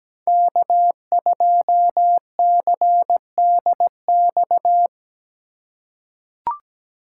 Morse code, heard as a keyed tone, about 17 wpm, 700 Hz. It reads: K2CDX
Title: K1USN SST QSO Practice - 17wpm